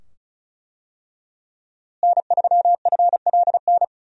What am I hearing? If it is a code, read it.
Morse code, heard as a keyed tone, about 36 wpm, 700 Hz. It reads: N3FLN
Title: USA Call Signs - 5 Characters - Repeated 2x - 35wpm